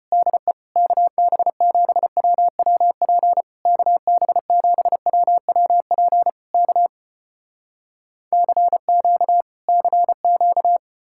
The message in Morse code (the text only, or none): DE KB7WWP KB7WWP K CQ CQ